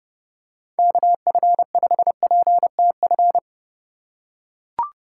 Morse code, heard as a keyed tone, about 31 wpm, 700 Hz. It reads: KF5PTF